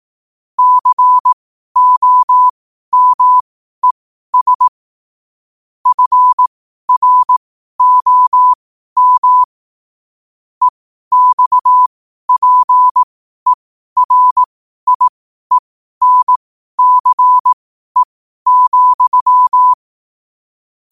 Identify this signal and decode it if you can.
Morse code, heard as a keyed tone, about 18 wpm, spaced as for 12 wpm, 1000 Hz. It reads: COMES FROM EXPERIENCE,
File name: qotdfiles / 12wpm